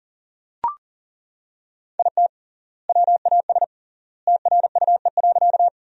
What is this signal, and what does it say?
Morse code, heard as a keyed tone, about 40 wpm, 700 Hz. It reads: IT WAS TRUE.
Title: 3x Word-Spacing - Sentences from Top 400 Words - 40wpm